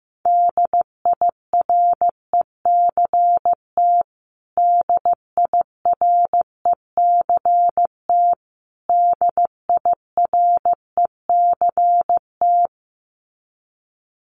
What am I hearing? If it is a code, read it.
Morse code, heard as a keyed tone, about 14 wpm, 700 Hz. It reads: DIRECT DIRECT DIRECT
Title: Top 500 Words Repeated 15wpm